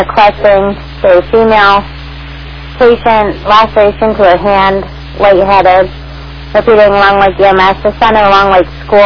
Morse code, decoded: none

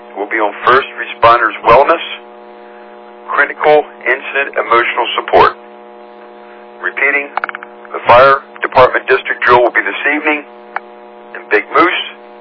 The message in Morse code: none

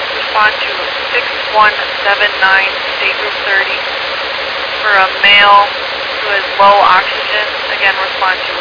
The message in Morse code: none

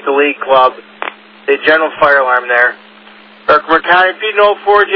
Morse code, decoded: none

